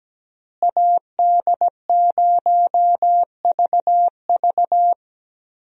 Morse code, heard as a keyed tone, about 17 wpm, 700 Hz. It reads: AD0VV
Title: USA Call Signs - 5 Characters - Repeated 2x - 17wpm